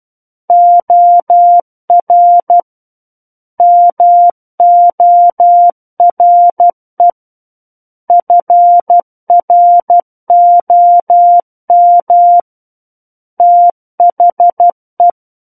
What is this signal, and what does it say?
Morse code, heard as a keyed tone, about 12 wpm, 700 Hz. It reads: OR MORE FROM THE
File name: Book One / 03-Chapter 3 - 12 WPM